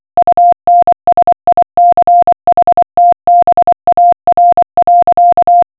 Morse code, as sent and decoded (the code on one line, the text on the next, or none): ..- -. ... .. -.-. .... - -... .- .-. .-.-.-
UNSICHTBAR.